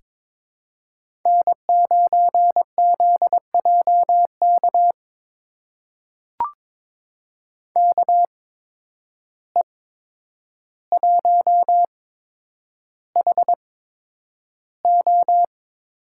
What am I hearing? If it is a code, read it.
Morse code, heard as a keyed tone, about 21 wpm, spaced as for 9 wpm, 700 Hz. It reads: N9ZJK KE1HO
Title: USA Call Signs - 5 Characters - 8x Character Spacing - 22wpm